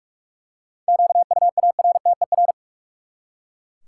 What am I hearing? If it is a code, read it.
Morse code, heard as a keyed tone, about 44 wpm, 700 Hz. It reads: QUARTER
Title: Seven Letter Words - 45wpm